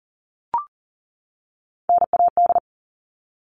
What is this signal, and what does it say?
Morse code, heard as a keyed tone, about 40 wpm, 700 Hz. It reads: NAD
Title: All Cut Numbers - 3-Digit 40wpm